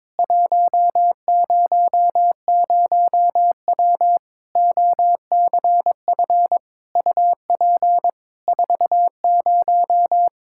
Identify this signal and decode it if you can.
Morse code, heard as a keyed tone, about 22 wpm, 700 Hz. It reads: 100W OCF UP 40
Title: K1USN SST QSO Practice 22wpm